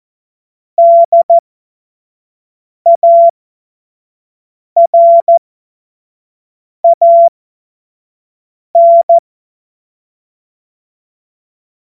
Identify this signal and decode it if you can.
Morse code, heard as a keyed tone, about 14 wpm, spaced as for 5 wpm, 680 Hz. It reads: DARAN
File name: Morsekurs / Witze / Witz 25. Juli 2023 25 F70